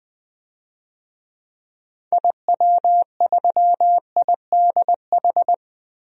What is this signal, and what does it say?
Morse code, heard as a keyed tone, about 20 wpm, 700 Hz. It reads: IW3IDH
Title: DXCC-100 Call Signs - 6 Characters - 20wpm